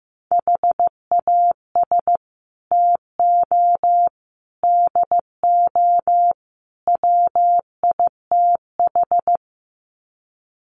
Morse code, heard as a keyed tone, about 14 wpm, 700 Hz. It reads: HAS TO DO WITH
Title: Sets of 4 Words 15wpm